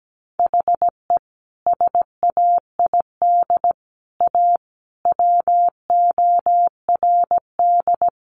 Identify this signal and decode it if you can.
Morse code, heard as a keyed tone, about 17 wpm, 700 Hz. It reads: HE SAID A WORD
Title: Sentences from Top 600 Words 17wpm